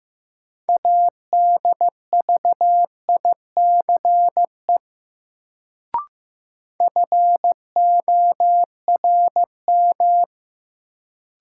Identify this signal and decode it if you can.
Morse code, heard as a keyed tone, about 15 wpm, 700 Hz. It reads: ADVICE FORM
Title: Sentences from Top 1100 Words - New Word Drill Alt - 15wpm